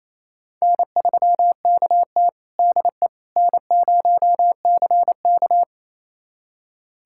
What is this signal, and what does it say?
Morse code, heard as a keyed tone, about 28 wpm, 700 Hz. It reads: N3KT DE N0CK
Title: USA Callsign DE Callsign - 4 Characters - 28wpm